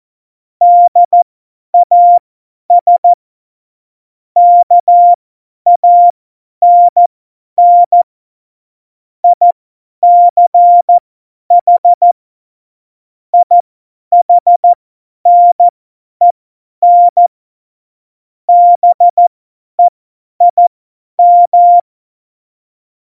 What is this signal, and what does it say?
Morse code, heard as a keyed tone, about 14 wpm, spaced as for 10 wpm, 700 Hz. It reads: DAS KANN ICH IHNEN BEIM
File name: Morsekurs / Witze / Witz 20. Oktober 2020, 50 F70